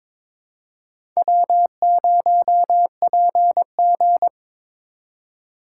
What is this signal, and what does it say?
Morse code, heard as a keyed tone, about 22 wpm, 700 Hz. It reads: W0PG
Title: USA Call Signs - 3x Character Spacing - 22wpm